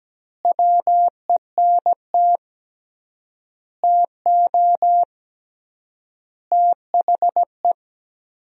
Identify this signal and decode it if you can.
Morse code, heard as a keyed tone, about 17 wpm, 700 Hz. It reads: WENT TO THE
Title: 3x Word-Spacing - Sentences from Top 400 Words - 17wpm